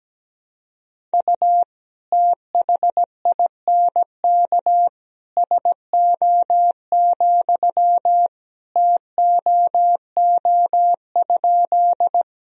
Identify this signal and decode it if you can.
Morse code, heard as a keyed tone, about 17 wpm, 700 Hz. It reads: U THINK SO, TOO?
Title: Sentences from Top 200 Words and Abbreviations 17wpm